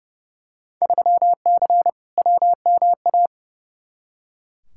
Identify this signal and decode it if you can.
Morse code, heard as a keyed tone, about 30 wpm, 700 Hz. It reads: 3C WMA
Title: Field Day Exchanges Encore - 30wpm